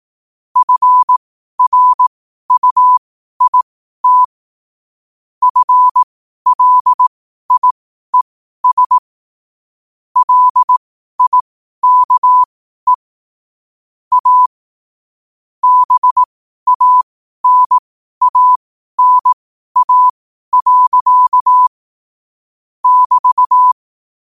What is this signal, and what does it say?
Morse code, heard as a keyed tone, about 18 wpm, spaced as for 12 wpm, 1000 Hz. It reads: FRUIT FLIES LIKE A BANANA. <BT>
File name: qotdfiles / 12wpm